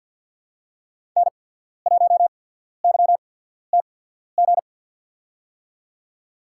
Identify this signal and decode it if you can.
Morse code, heard as a keyed tone, about 51 wpm, spaced as for 14 wpm, 700 Hz. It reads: N1YTG
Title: USA Call Signs - 5 Characters - 8x Character Spacing - 50wpm